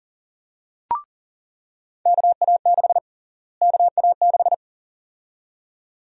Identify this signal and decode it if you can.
Morse code, heard as a keyed tone, about 40 wpm, 700 Hz. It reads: KA6 KA6
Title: USA Call Sign Prefixes - Repeated 2x - 40wpm